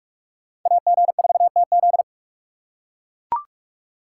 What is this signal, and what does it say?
Morse code, heard as a keyed tone, about 44 wpm, 700 Hz. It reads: AG4TZ